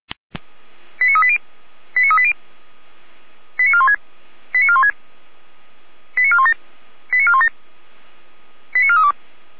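Five-tone selective call (ZVEI), heard as quick runs of five tones, tones of about 70 ms. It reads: 89380 89380 89416 89416 89417 89417 89432
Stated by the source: ZVEI five-tone alert received by SDR.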